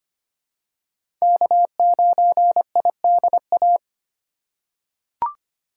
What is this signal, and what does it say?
Morse code, heard as a keyed tone, about 25 wpm, 700 Hz. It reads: K9IDA
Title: USA Call Signs - 5 Characters - 4x Character Spacing - 25wpm